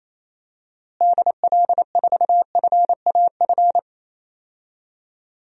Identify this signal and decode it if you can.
Morse code, heard as a keyed tone, about 28 wpm, 700 Hz. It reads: DL4FAF
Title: DXCC-100 Call Signs - 6 Characters - Repeated 3x - 28wpm